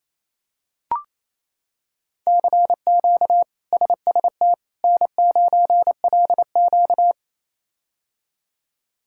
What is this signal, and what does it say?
Morse code, heard as a keyed tone, about 27 wpm, 700 Hz. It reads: CQ SST N9LQ
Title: K1USN SST QSO Practice - 28wpm